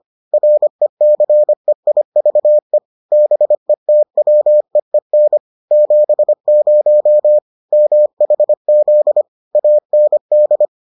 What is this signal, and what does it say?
Morse code, heard as a keyed tone, about 25 wpm, 600 Hz. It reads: RECEIVE BETWEEN 70 MHZ AND